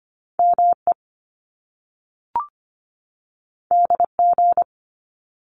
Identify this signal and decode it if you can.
Morse code, heard as a keyed tone, about 25 wpm, 700 Hz. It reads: ME DG
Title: Top Two Letter Combinations in Words 25wpm